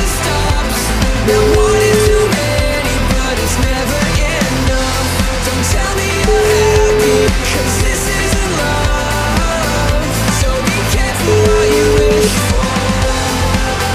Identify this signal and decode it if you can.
Ringback tone (Europe, 425 Hz), with noise.